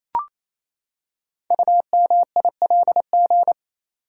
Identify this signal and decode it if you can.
Morse code, heard as a keyed tone, about 29 wpm, 700 Hz. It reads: UMILG
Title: Code Groups - 5 Letters - 28wpm